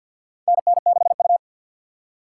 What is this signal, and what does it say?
Morse code, heard as a keyed tone, about 52 wpm, 700 Hz. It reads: NN6U
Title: USA Call Signs - 4 Characters - 3x Character Spacing - 50wpm